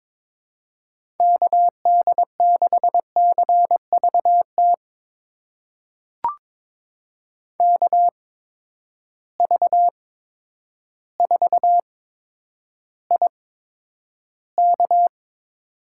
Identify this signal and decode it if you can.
Morse code, heard as a keyed tone, about 22 wpm, spaced as for 10 wpm, 700 Hz. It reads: KD6CVT KV4IK
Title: USA Call Signs - 8x Character Spacing - 22wpm